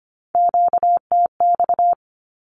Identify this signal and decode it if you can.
Morse code, heard as a keyed tone, about 25 wpm, 700 Hz. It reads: QTX
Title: Q-Codes 25wpm